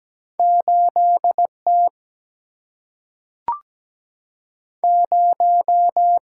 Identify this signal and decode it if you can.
Morse code, heard as a keyed tone, about 17 wpm, 700 Hz. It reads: 8T 0
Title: Single-Letter Single-Number 17wpm